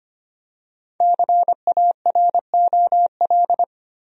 Morse code, heard as a keyed tone, about 25 wpm, 700 Hz. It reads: CAROL
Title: USA Common Names - Max 5 Letters - 25wpm